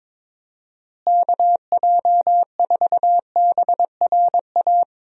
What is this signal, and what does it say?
Morse code, heard as a keyed tone, about 21 wpm, 700 Hz. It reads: KJ4BRA
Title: USA Call Signs - 6 Characters - Repeated 3x - 22wpm